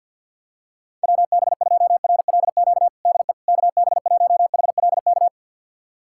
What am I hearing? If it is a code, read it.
Morse code, heard as a keyed tone, about 53 wpm, 700 Hz. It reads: WB2RLQ DE KB1HLK
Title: USA Callsign DE Callsign - 6 Characters - 50wpm